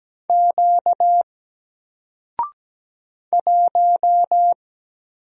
Morse code, heard as a keyed tone, about 17 wpm, 700 Hz. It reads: Q 1
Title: Single Character (Letter:Number:Punctuation:Prosign) 17wpm